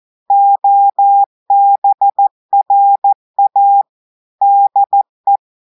Morse code, heard as a keyed tone, about 14 wpm, 800 Hz. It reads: OBRA DE
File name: audios / portugais / portugais texte en clair 1 vitesse 14 mots minutes